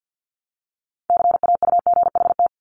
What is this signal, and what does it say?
Morse code, heard as a keyed tone, about 50 wpm, 700 Hz. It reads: CAUGHT
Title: Multi-letter Phonemes 50wpm